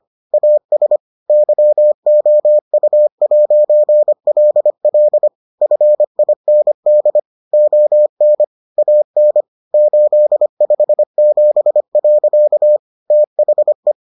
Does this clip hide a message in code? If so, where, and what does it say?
Yes: Morse code from 0.3 to 13.9 s, heard as a keyed tone, about 25 wpm, 600 Hz. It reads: AS YOU'LL FIND ON AN 857. THE